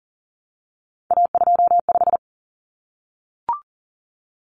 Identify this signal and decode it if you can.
Morse code, heard as a keyed tone, about 40 wpm, 700 Hz. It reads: A25
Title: Common Cut Numbers - 3-Digit 40wpm